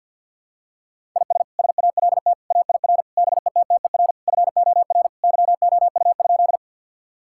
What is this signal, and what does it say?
Morse code, heard as a keyed tone, about 52 wpm, 700 Hz. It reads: IS SALT AIR BETTER FOR YOU?
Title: Sentences from Top 1000 Words - Review Part 5 - 50wpm